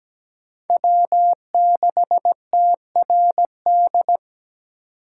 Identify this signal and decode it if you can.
Morse code, heard as a keyed tone, about 17 wpm, 700 Hz. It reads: W6TRD
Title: USA Call Signs - 5 Characters - 4x Character Spacing - 17wpm